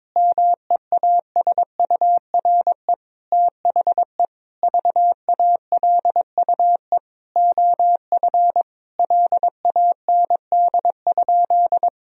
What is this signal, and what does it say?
Morse code, heard as a keyed tone, about 22 wpm, 700 Hz. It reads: MEASURE THE VALUE OF LAND?
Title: Sentences from Top 1000 Words - Review Part 3 - 22wpm